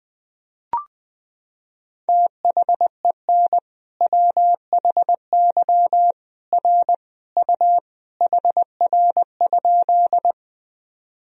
Morse code, heard as a keyed tone, about 19 wpm, 700 Hz. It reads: THEN WHY R U HR?
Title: Sentences from Top 200 Words and Abbreviations 20wpm